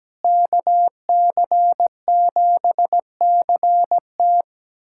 Morse code, heard as a keyed tone, about 17 wpm, 700 Hz. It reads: KC7CT
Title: USA Callsign DE Callsign - 5 Characters - 17wpm